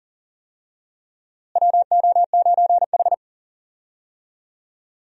Morse code, heard as a keyed tone, about 40 wpm, 700 Hz. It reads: WO9H
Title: USA Call Signs - 4 Characters - Repeated 3x - 40wpm